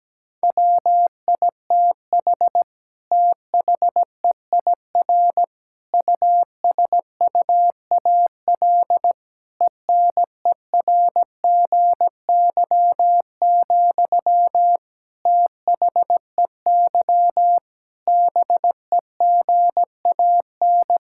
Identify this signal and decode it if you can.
Morse code, heard as a keyed tone, about 17 wpm, 700 Hz. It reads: WITH THEIR USUAL ENERGY, THEY BEGAN
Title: Sentences from Top 1000 Words - Review Part 6 - 17wpm